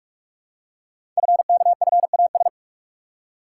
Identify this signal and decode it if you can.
Morse code, heard as a keyed tone, about 44 wpm, 700 Hz. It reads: FKFAS
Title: Code Groups - 5 Letter:Numbers - 45wpm